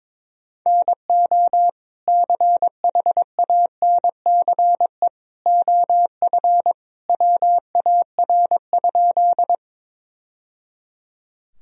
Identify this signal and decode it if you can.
Morse code, heard as a keyed tone, about 22 wpm, 700 Hz. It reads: NO CHANCE OF WAR?
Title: Sentences from Top 1000 Words - Review Part 2 - 22wpm